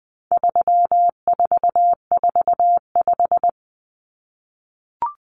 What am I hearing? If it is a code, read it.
Morse code, heard as a keyed tone, about 20 wpm, 700 Hz. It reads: 3445
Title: Numbers - 4-Digit 20wpm